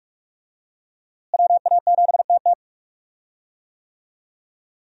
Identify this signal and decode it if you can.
Morse code, heard as a keyed tone, about 42 wpm, 700 Hz. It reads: WA7TT
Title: USA Call Signs - 5 Characters - Repeated 3x - 45wpm